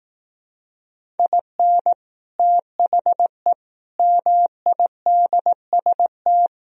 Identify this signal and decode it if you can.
Morse code, heard as a keyed tone, about 18 wpm, 700 Hz. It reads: IN THE MIDST
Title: Speed-Racing - Sets of 5 Words 15wpm